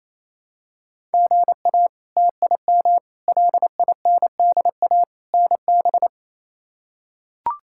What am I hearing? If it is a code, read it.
Morse code, heard as a keyed tone, about 28 wpm, 700 Hz. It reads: GA TIM LINDA NB
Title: K1USN SST QSO Practice - 28wpm